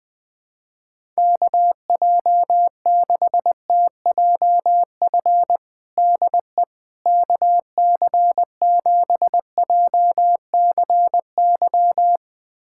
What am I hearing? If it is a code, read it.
Morse code, heard as a keyed tone, about 20 wpm, 700 Hz. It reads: KJ6TJF DE KC7JCY
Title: USA Callsign DE Callsign - 6 Characters - 20wpm